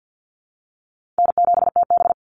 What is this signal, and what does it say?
Morse code, heard as a keyed tone, about 48 wpm, 700 Hz. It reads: N7TB
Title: Easy Call Signs Repeated 2x - 50wpm